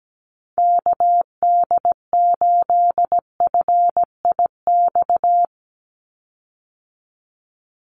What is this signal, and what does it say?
Morse code, heard as a keyed tone, about 17 wpm, 700 Hz. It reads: KD8FIX
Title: USA Call Signs 17wpm